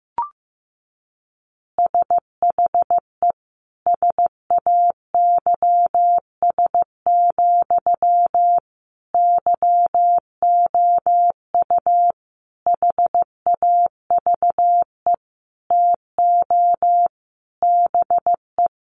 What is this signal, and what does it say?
Morse code, heard as a keyed tone, about 15 wpm, 700 Hz. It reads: SHE SAYS, YOU HAVE TO BE
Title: Easy Phrases 15wpm